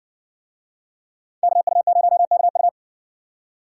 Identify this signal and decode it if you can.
Morse code, heard as a keyed tone, about 57 wpm, 700 Hz. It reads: KU0KU